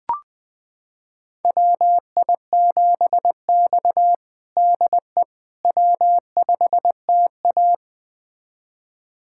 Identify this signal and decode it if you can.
Morse code, heard as a keyed tone, about 21 wpm, 700 Hz. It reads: WI7X DE W5TA
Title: USA Callsign DE Callsign - 4 Characters - 20wpm